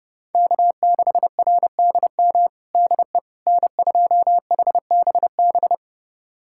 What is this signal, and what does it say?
Morse code, heard as a keyed tone, about 30 wpm, 700 Hz. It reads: K6RDM DE N2HBB
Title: USA Callsign DE Callsign - 5 Characters - 30wpm